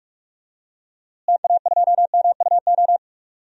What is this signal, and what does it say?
Morse code, heard as a keyed tone, about 46 wpm, 700 Hz. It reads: TA2MUO
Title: DXCC-100 Call Signs - 6 Characters - Repeated 3x - 45wpm